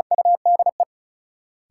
Morse code, heard as a keyed tone, about 36 wpm, 700 Hz. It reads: UDE
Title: Sentences from Top 1000 Words - Review Part 3 - New Word Drill - 35wpm